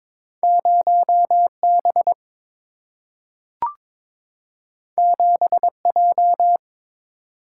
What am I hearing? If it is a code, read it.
Morse code, heard as a keyed tone, about 22 wpm, 700 Hz. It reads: 0B 7J
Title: Single-Letter Single-Number 22wpm